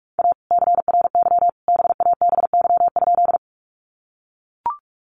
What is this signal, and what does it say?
Morse code, heard as a keyed tone, about 44 wpm, 700 Hz. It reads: A CRY BABY?
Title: Sentences from Top 1300 Words 45wpm